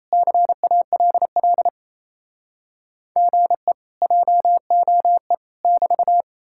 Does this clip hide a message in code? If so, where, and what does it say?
Yes: Morse code from 0.1 to 6.2 s, heard as a keyed tone, about 38 wpm, 700 Hz. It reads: CALL GE JOE <BT>